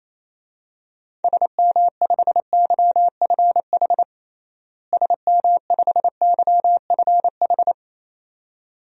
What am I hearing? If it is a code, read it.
Morse code, heard as a keyed tone, about 28 wpm, 700 Hz. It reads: SM5YFH SM5YFH
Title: DXCC-100 Call Signs - 6 Characters - Repeated 2x - 28wpm